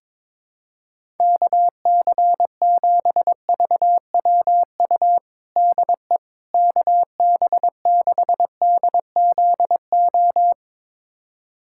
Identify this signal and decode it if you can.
Morse code, heard as a keyed tone, about 22 wpm, 700 Hz. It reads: KC7VWU DE KB6DZO
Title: USA Callsign DE Callsign 22wpm